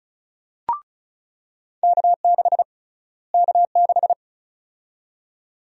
Morse code, heard as a keyed tone, about 36 wpm, 700 Hz. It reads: K6 K6
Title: USA Call Sign Prefixes - Repeated 2x - 35wpm